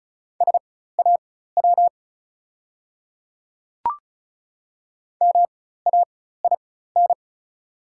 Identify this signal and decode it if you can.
Morse code, heard as a keyed tone, about 34 wpm, spaced as for 16 wpm, 700 Hz. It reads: SAW MAIN
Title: 4x Character Spacing - Top 300 Words - 35wpm